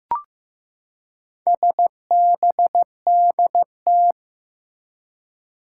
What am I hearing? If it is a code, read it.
Morse code, heard as a keyed tone, about 15 wpm, 700 Hz. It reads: SBDT
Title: Four Challenging Letter-Number Combinations 15wpm